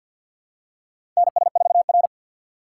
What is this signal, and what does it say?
Morse code, heard as a keyed tone, about 53 wpm, 700 Hz. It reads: NS4R
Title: USA Call Signs - 4 Characters - 3x Character Spacing - 50wpm